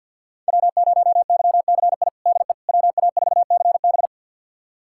Morse code, heard as a keyed tone, about 47 wpm, 700 Hz. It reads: W0YCI DE WA4CB
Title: USA Callsign DE Callsign - 5 Characters - 50wpm